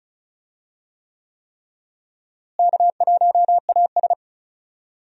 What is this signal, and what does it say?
Morse code, heard as a keyed tone, about 36 wpm, 700 Hz. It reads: K1AS
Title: USA Call Signs - 4 Characters - Repeated 3x - 35wpm